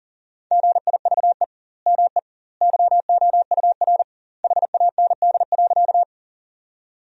Morse code, heard as a keyed tone, about 40 wpm, 700 Hz. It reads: GIVE ME YOUR HAND.
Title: 2x Word-Spacing - Sentences from Top 200 Words - 40wpm